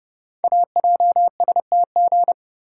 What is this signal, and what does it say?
Morse code, heard as a keyed tone, about 30 wpm, 690 Hz. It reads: AJSTG
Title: Code Groups - 5 Letters - 30wpm